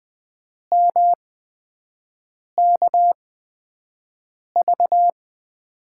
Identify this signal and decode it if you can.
Morse code, heard as a keyed tone, about 20 wpm, spaced as for 5 wpm, 700 Hz. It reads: MKV
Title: USA Call Signs - 6 Characters - 8x Character Spacing - 20wpm